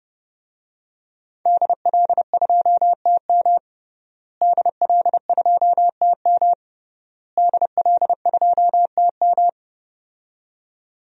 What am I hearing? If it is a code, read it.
Morse code, heard as a keyed tone, about 30 wpm, 700 Hz. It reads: DL2TM DL2TM DL2TM
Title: DXCC-100 Call Signs - 5 Characters - Repeated 3x - 30wpm